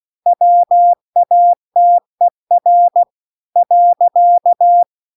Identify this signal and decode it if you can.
Morse code, heard as a keyed tone, about 16 wpm, 700 Hz. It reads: WATER .